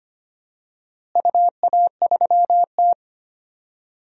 Morse code, heard as a keyed tone, about 25 wpm, 700 Hz. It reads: UA3T
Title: DXCC-100 Call Signs - 4 Characters - Repeated 2x - 25wpm